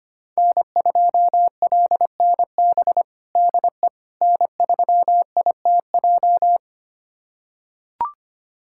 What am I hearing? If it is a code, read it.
Morse code, heard as a keyed tone, about 25 wpm, 700 Hz. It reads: N2LNB DE N3ITJ
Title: USA Callsign DE Callsign - 5 Characters - 25wpm